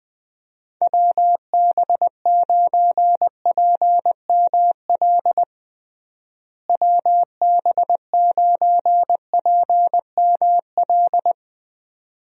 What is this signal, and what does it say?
Morse code, heard as a keyed tone, about 20 wpm, 700 Hz. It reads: WB9PML WB9PML
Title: USA Call Signs - 6 Characters - Repeated 2x - 20wpm